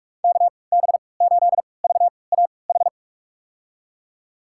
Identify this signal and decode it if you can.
Morse code, heard as a keyed tone, about 47 wpm, spaced as for 26 wpm, 700 Hz. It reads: KB8VAH
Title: USA Call Signs - 3x Character Spacing - 45wpm